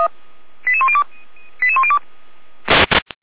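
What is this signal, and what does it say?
Five-tone selective call (ZVEI), heard as quick runs of five tones, tones of about 70 ms. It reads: 88102 88102
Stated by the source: ZVEI five-tone call 88103 received by SDR, recorded by feuerwehr-orlamuende.org fire brigade server.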